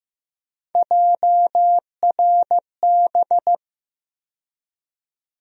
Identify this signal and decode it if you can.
Morse code, heard as a keyed tone, about 15 wpm, 700 Hz. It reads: JRB
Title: USA Call Sign Postfixes - 3 Letters 15wpm